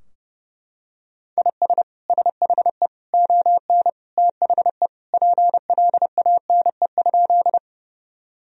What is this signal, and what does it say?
Morse code, heard as a keyed tone, about 32 wpm, 700 Hz. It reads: IS SHE ON THE PLANE?
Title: Sentences from Top 1000 Words - Review Part 3 - 30wpm